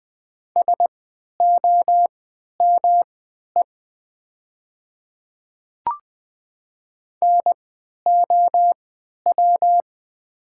Morse code, heard as a keyed tone, about 20 wpm, spaced as for 11 wpm, 700 Hz. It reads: SOME NOW
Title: 3x Character Spacing - Top 100 Words - 20wpm